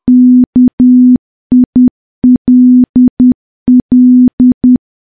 Morse code, heard as a keyed tone, about 10 wpm, 250 Hz. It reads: KILL
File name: morse / morse-training-10WPM